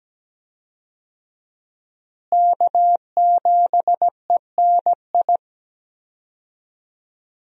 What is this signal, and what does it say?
Morse code, heard as a keyed tone, about 17 wpm, 700 Hz. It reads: K7ENI
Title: USA Call Signs - 5 Characters - Repeated 2x - 17wpm